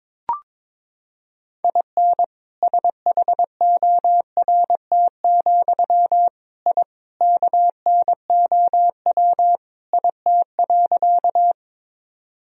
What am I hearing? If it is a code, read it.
Morse code, heard as a keyed tone, about 22 wpm, 700 Hz. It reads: IN SHORT, I KNOW IT.
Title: Sentences from Top 1000 Words - Review Part 2 - 22wpm